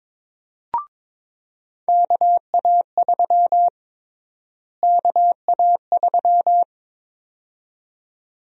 Morse code, heard as a keyed tone, about 22 wpm, 700 Hz. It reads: KA3 KA3
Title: USA Call Sign Prefixes - Repeated 2x - 22wpm